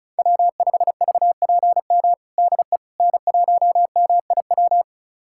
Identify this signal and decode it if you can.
Morse code, heard as a keyed tone, about 34 wpm, 700 Hz. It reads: W5VPM DE N1MIW
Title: USA Callsign DE Callsign - 5 Characters - 35wpm